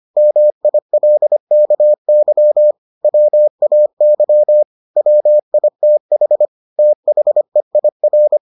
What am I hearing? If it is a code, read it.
Morse code, heard as a keyed tone, about 25 wpm, 600 Hz. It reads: MILKY WAY WITH THEIR